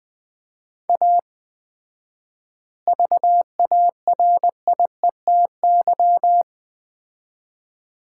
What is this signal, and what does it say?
Morse code, heard as a keyed tone, about 20 wpm, 700 Hz. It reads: A VARIETY
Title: 4x Word-Spacing - Sets of 3 Words - 20wpm